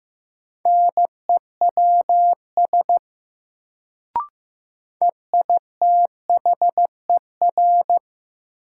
Morse code, heard as a keyed tone, about 16 wpm, 700 Hz. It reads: NEWS EITHER
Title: Sets of 5 Words - New Word Drill - 15wpm